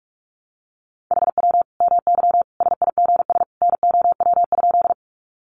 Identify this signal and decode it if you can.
Morse code, heard as a keyed tone, about 44 wpm, 700 Hz. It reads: HW MY SIGS NOW?
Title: Common QSO Phrases 45wpm